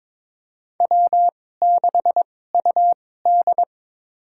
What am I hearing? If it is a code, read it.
Morse code, heard as a keyed tone, about 22 wpm, spaced as for 16 wpm, 700 Hz. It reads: W6UD
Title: USA Call Signs - 4 Characters - 2x Character Spacing - 22wpm